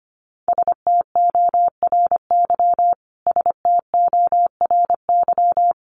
Morse code, heard as a keyed tone, about 25 wpm, 700 Hz. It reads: STORY STORY
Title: Top 500 Words Repeated 25wpm